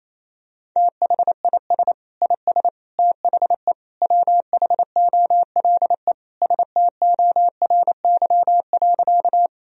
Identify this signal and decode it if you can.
Morse code, heard as a keyed tone, about 28 wpm, 700 Hz. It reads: THIS IS THE WHOLE STORY.
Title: Sentences from Top 1000 Words - Review Part 1 - 28wpm